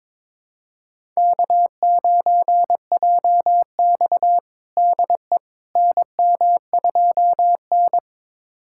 Morse code, heard as a keyed tone, about 22 wpm, 700 Hz. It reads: K9JX DE NM2N
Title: USA Callsign DE Callsign - 4 Characters - 22wpm